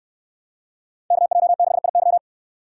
Morse code, heard as a keyed tone, about 68 wpm, 700 Hz. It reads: KJ6IY